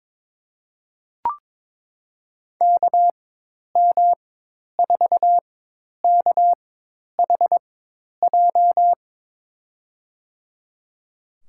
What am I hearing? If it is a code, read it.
Morse code, heard as a keyed tone, about 22 wpm, spaced as for 10 wpm, 700 Hz. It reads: KM4KHJ
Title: USA Call Signs - 4x Character Spacing - 22wpm